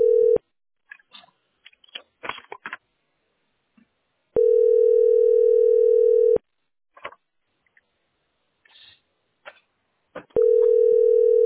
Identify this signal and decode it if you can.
Ringback tone (North America), 440 + 480 Hz.